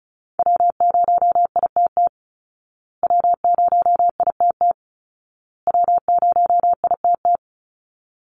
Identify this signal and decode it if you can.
Morse code, heard as a keyed tone, about 34 wpm, 700 Hz. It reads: W0ITT W0ITT W0ITT
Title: Call Signs Repeated 2x - 35wpm